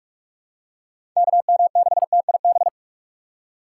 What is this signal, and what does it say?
Morse code, heard as a keyed tone, about 42 wpm, 700 Hz. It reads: KM6TIB